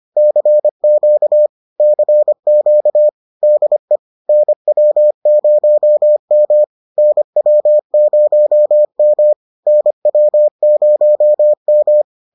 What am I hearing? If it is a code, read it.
Morse code, heard as a keyed tone, about 25 wpm, 600 Hz. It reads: CQ CQ DE NW0M NW0M NW0M